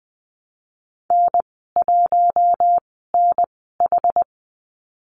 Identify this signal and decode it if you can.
Morse code, heard as a keyed tone, about 20 wpm, spaced as for 14 wpm, 700 Hz. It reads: N1NH
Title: 076 Lesson 13g - Letter L - Call Signs 20wpm